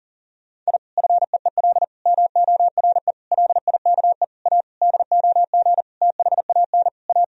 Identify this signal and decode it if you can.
Morse code, heard as a keyed tone, about 40 wpm, 700 Hz. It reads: I FEEL MORE LIKE A DOG THAN A
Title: Sentences from Top 1000 Words - Review Part 2 - 40wpm